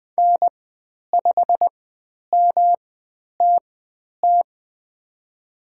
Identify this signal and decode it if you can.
Morse code, heard as a keyed tone, about 20 wpm, spaced as for 10 wpm, 700 Hz. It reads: N5MTT
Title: 109 Lesson 18g - Letter m - Call Signs 20wpm